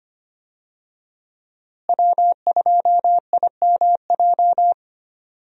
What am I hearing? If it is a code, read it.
Morse code, heard as a keyed tone, about 25 wpm, 700 Hz. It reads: W2IMJ
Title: USA Call Signs - 5 Characters - Repeated 3x - 25wpm